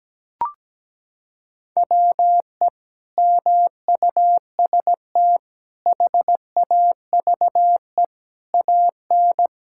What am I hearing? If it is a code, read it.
Morse code, heard as a keyed tone, about 17 wpm, 700 Hz. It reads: WE MUST HAVE AN
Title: Sentences from Top 1000 Words - Review Part 6 - 17wpm